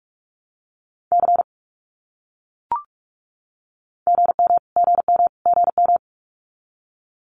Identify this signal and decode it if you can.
Morse code, heard as a keyed tone, about 44 wpm, 700 Hz. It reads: C GM GM GM
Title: Abbreviations Repeated 3x - 45wpm